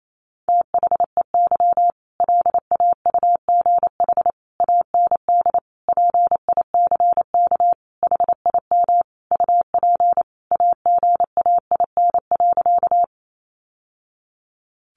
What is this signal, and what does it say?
Morse code, heard as a keyed tone, about 28 wpm, 700 Hz. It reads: THEY LAUGH AND PICK HIM UP AGAIN.
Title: Sentences from Top 1000 Words 28wpm